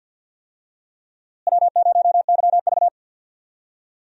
Morse code, heard as a keyed tone, about 52 wpm, 700 Hz. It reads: W0YV
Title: USA Call Signs - 4 Characters - Repeated 2x - 50wpm